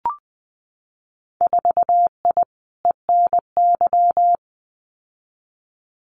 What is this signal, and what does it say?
Morse code, heard as a keyed tone, about 20 wpm, 700 Hz. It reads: 4I ENY
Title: Winter Field Day Exchanges Encore 20wpm